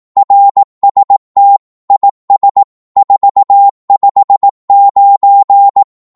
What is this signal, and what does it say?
Morse code, heard as a keyed tone, about 18 wpm, 800 Hz. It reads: RST IS 459
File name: Morse / qso-18wpm